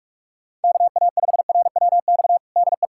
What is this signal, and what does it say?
Morse code, heard as a keyed tone, about 47 wpm, 700 Hz. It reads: KA5RWX DE
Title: USA Callsign DE Callsign - 6 Characters - 45wpm